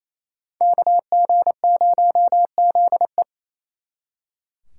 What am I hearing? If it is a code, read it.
Morse code, heard as a keyed tone, about 33 wpm, 700 Hz. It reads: KG0ZE
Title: Speed Racing - USA 2x2 Call Signs 28wpm